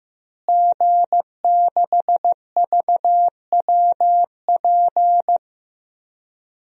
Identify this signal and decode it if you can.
Morse code, heard as a keyed tone, about 15 wpm, 700 Hz. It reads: G6VWP
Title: DXCC-100 Call Signs - 5 Characters - Repeated 2x - 15wpm